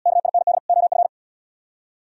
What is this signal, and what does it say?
Morse code, heard as a keyed tone, about 73 wpm, 700 Hz. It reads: KIND OF